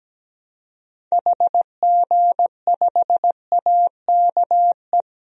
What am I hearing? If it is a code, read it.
Morse code, heard as a keyed tone, about 17 wpm, 700 Hz. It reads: HG5AKE
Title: DXCC-100 Call Signs - 6 Characters - 17wpm